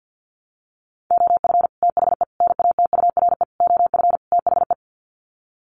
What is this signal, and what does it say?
Morse code, heard as a keyed tone, about 50 wpm, 700 Hz. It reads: OF THE NATURE OF THE
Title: Sets of 5 Words Encore2 50wpm